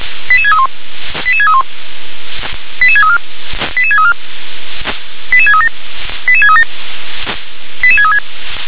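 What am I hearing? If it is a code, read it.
Five-tone selective call (ZVEI), heard as quick runs of five tones, tones of about 80 ms. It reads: 88631 88631 88634 88634 88637 88637 88636
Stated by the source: ZVEI five-tone selective call received on SDR.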